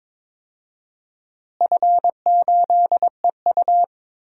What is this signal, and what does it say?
Morse code, heard as a keyed tone, about 22 wpm, 700 Hz. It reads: F8EU
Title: DXCC-100 Call Signs - 4 Characters - Repeated 2x - 22wpm